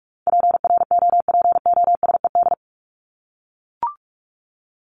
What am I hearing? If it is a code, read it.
Morse code, heard as a keyed tone, about 44 wpm, 700 Hz. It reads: PROPOSED
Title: ED Suffix - Top 100 Words with Base Words 45wpm